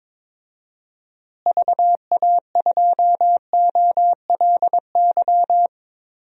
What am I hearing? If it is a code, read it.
Morse code, heard as a keyed tone, about 22 wpm, 700 Hz. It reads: VA2OLY
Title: DXCC-100 Call Signs - 6 Characters - Repeated 2x - 22wpm